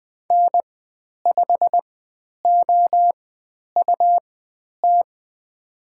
Morse code, heard as a keyed tone, about 20 wpm, spaced as for 10 wpm, 700 Hz. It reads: N5OUT
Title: 109 Lesson 18g - Letter m - Call Signs 20wpm